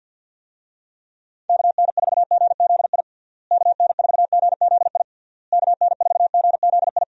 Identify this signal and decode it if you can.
Morse code, heard as a keyed tone, about 51 wpm, 700 Hz. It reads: KN4GZI KN4GZI KN4GZI
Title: USA Call Signs - 6 Characters - Repeated 3x - 50wpm